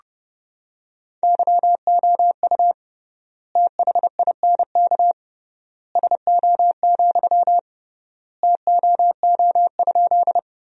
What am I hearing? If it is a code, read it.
Morse code, heard as a keyed tone, about 32 wpm, 700 Hz. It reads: YOU THINK SO, TOO?
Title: 3x Word-Spacing - Sentences from Top 200 Words - 30wpm